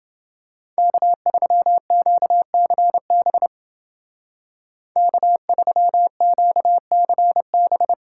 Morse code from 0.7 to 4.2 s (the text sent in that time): K3QCB